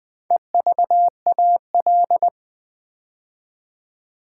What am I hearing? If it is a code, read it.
Morse code, heard as a keyed tone, about 20 wpm, 700 Hz. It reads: EVAL
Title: Build A Word - 4 Letters At A Time - 8-Letter Words - 20wpm